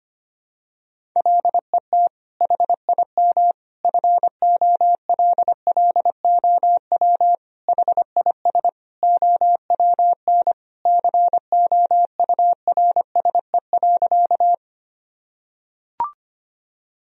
Morse code, heard as a keyed tone, about 25 wpm, 700 Hz. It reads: LET HIM FOLLOW HIS OWN COURSE.
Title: Sentences from Top 1000 Words - Review Part 1 - 25wpm